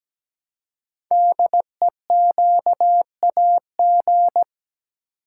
Morse code, heard as a keyed tone, about 17 wpm, 700 Hz. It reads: DEQAG
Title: Code Groups - 5 Letters - 17wpm